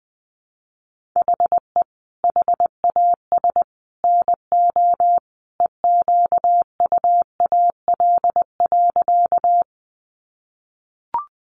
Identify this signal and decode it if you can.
Morse code, heard as a keyed tone, about 20 wpm, 700 Hz. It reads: HE HAS NO EQUAL.
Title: Sentences from Top 1400 Words 20wpm